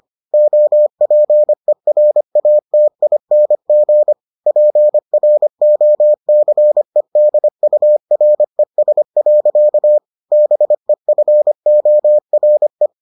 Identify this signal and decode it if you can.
Morse code, heard as a keyed tone, about 25 wpm, 600 Hz. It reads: OPERATING PROCEDURES. BEFORE